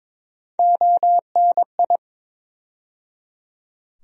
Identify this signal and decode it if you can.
Morse code, heard as a keyed tone, about 22 wpm, 700 Hz. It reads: ONI
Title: USA Call Sign Postfixes - 3 Letters 22wpm